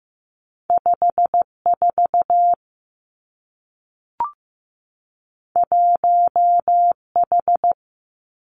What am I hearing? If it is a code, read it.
Morse code, heard as a keyed tone, about 15 wpm, 700 Hz. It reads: 5 4 1 H
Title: Two Challenging Letter-Number Combinations 15wpm